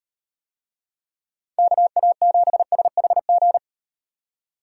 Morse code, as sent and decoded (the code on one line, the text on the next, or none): -.- .- --... ... .... --.
KA7SHG